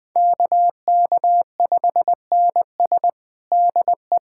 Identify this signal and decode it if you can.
Morse code, heard as a keyed tone, about 20 wpm, 700 Hz. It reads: KK5NS DE
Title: USA Callsign DE Callsign 20wpm